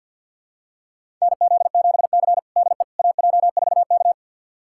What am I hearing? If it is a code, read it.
Morse code, heard as a keyed tone, about 52 wpm, 700 Hz. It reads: NZ7C DE AJ4K
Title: USA Callsign DE Callsign - 4 Characters - 50wpm